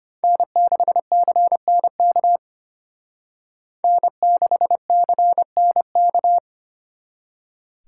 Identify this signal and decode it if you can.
Morse code, heard as a keyed tone, about 33 wpm, 700 Hz. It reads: N6CNK N6CNK